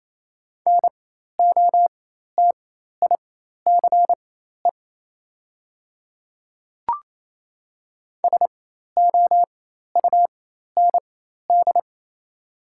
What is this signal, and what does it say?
Morse code, heard as a keyed tone, about 28 wpm, spaced as for 13 wpm, 700 Hz. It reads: NOTICE SOUND
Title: 4x Character Spacing - Top 500 Words - 28wpm